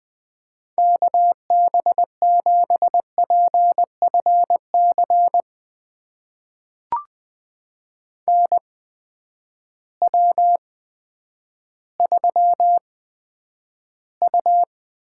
Morse code, heard as a keyed tone, about 20 wpm, spaced as for 10 wpm, 700 Hz. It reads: KB7PFC NW3U
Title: USA Call Signs - 8x Character Spacing - 20wpm